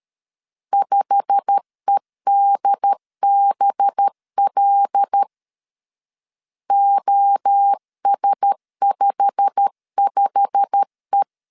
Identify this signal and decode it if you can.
Morse code, heard as a keyed tone, about 13 wpm, 800 Hz. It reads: 5EDBL OS55E
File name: LEZIONI / 8 YJ45 / 8 YJ45 12wpm 2151